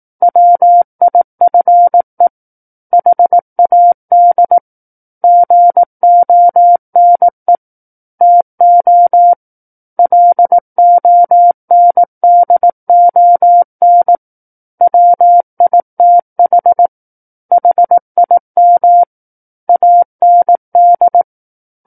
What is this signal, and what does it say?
Morse code, heard as a keyed tone, about 18 wpm, 700 Hz. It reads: WIFE HAD GONE TO LONDON WITH HIM AND
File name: 18 WPM